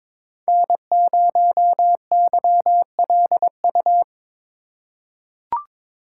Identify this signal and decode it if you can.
Morse code, heard as a keyed tone, about 22 wpm, 700 Hz. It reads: N0YLU
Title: USA Call Signs - 5 Characters - 4x Character Spacing - 22wpm